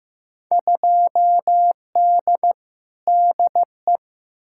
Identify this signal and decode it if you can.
Morse code, heard as a keyed tone, about 15 wpm, 700 Hz. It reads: 2D DE
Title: Field Day Exchanges Encore - 15wpm